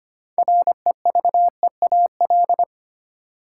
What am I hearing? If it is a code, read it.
Morse code, heard as a keyed tone, about 25 wpm, 700 Hz. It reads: REVEAL
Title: Six Letter Words - 25wpm